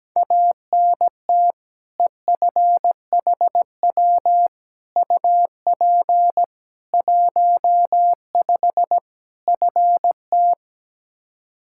Morse code, heard as a keyed tone, about 17 wpm, 700 Hz. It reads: ANT EFHW UP 15 FT
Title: Practice QSO - Exchange 2 - Info - 17wpm